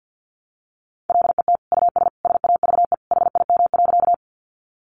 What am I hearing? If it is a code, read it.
Morse code, heard as a keyed tone, about 50 wpm, 700 Hz. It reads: LET US SAVE HIM.
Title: Sentences from Top 1100 Words 50wpm